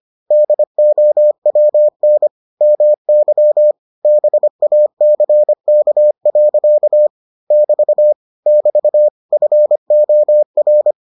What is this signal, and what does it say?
Morse code, heard as a keyed tone, about 25 wpm, 600 Hz. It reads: DOWN MY BACK. <BT> <BT> FOR